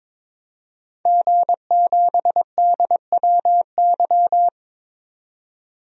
Morse code, heard as a keyed tone, about 22 wpm, 700 Hz. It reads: G7DWY
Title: DXCC-100 Call Signs - 5 Characters - 22wpm